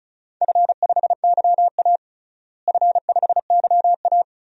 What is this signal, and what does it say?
Morse code, heard as a keyed tone, about 34 wpm, 700 Hz. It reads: F5YA F5YA
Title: DXCC-100 Call Signs - 4 Characters - Repeated 2x - 35wpm